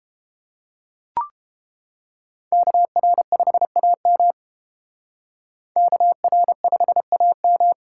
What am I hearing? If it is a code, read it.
Morse code, heard as a keyed tone, about 29 wpm, 700 Hz. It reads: KR5AM KR5AM